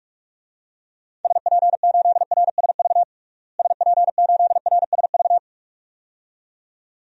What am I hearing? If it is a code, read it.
Morse code, heard as a keyed tone, about 46 wpm, 700 Hz. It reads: SP8RSV SP8RSV
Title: DXCC-100 Call Signs - 6 Characters - Repeated 2x - 45wpm